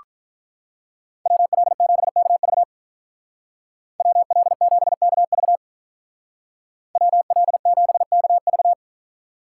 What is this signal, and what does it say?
Morse code, heard as a keyed tone, about 47 wpm, 700 Hz. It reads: WL7KV WL7KV WL7KV